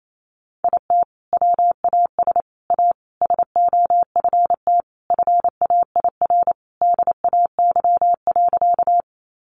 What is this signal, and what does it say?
Morse code, heard as a keyed tone, about 28 wpm, 700 Hz. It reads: IT WAS A SOFT FAIR DAY.
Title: Sentences from Top 1500 Words 28wpm